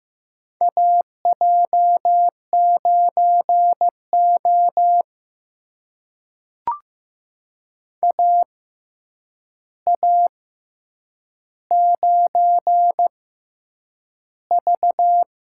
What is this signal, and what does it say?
Morse code, heard as a keyed tone, about 15 wpm, spaced as for 8 wpm, 700 Hz. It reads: AJ9O AA9V
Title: USA Call Signs - 4 Characters - 6x Character Spacing - 15wpm